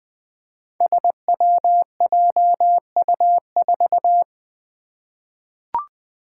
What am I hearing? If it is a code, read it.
Morse code, heard as a keyed tone, about 20 wpm, 700 Hz. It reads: SWJU4
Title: Five Challenging Letter-Number Combinations 20wpm